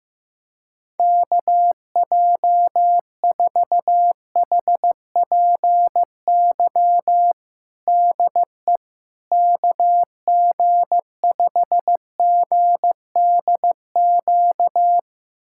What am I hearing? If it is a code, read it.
Morse code, heard as a keyed tone, about 15 wpm, 700 Hz. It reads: KJ4HPY DE KG5GDQ
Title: USA Callsign DE Callsign 15wpm